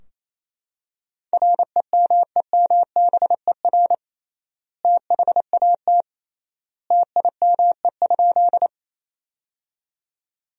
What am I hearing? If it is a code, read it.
Morse code, heard as a keyed tone, about 28 wpm, 700 Hz. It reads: REMEMBER THAT TIME?
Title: 3x Word-Spacing - Sentences from Top 400 Words - 28wpm